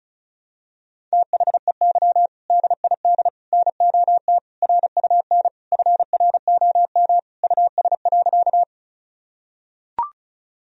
Morse code, heard as a keyed tone, about 36 wpm, 700 Hz. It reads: THEY DID NOT RUN FROM US.
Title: Sentences from Top 1000 Words - Review Part 1 - 35wpm